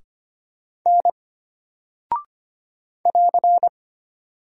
Morse code, heard as a keyed tone, about 25 wpm, 700 Hz. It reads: N <AR>
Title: Single Character (Letter:Number:Punctuation:Prosign) Alt 25wpm